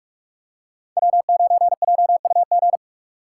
Morse code, heard as a keyed tone, about 43 wpm, 700 Hz. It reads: W9JUG
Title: K1USN SST QSO Practice - 45wpm